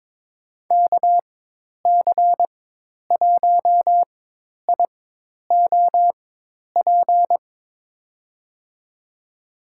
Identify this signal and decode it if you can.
Morse code, heard as a keyed tone, about 22 wpm, spaced as for 10 wpm, 700 Hz. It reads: KC1IOP
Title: USA Call Signs - 6 Characters - 4x Character Spacing - 22wpm